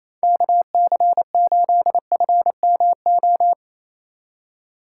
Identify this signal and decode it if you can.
Morse code, heard as a keyed tone, about 36 wpm, 700 Hz. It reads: KC8FMO